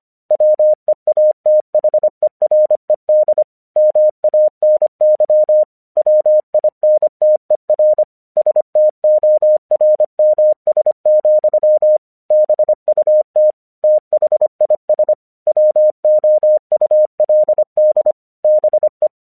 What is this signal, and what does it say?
Morse code, heard as a keyed tone, about 25 wpm, 610 Hz. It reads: WEATHERED MANY WINTER STORMS, BUT THIS WOULD BE
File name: 1 English 25wpm / AQC05